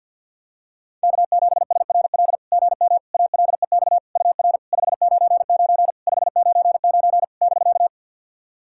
Morse code, heard as a keyed tone, about 50 wpm, 690 Hz. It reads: K7SRL GM ALEX UR 599 599 <BK>
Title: SOTA QSO Practice 50wpm